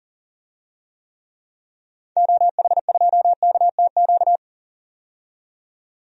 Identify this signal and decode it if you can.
Morse code, heard as a keyed tone, about 40 wpm, 700 Hz. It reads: OH2KTQ
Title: DXCC-100 Call Signs - 6 Characters - 40wpm